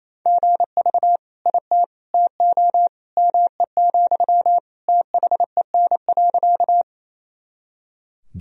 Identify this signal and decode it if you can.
Morse code, heard as a keyed tone, about 28 wpm, 700 Hz. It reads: GV IT TO ME, THEN.
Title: Sentences from Top 200 Words and Abbreviations 28wpm